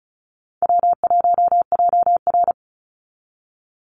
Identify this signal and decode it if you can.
Morse code, heard as a keyed tone, about 34 wpm, 700 Hz. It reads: W1JR
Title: Easy Call Signs Repeated 3x - 35wpm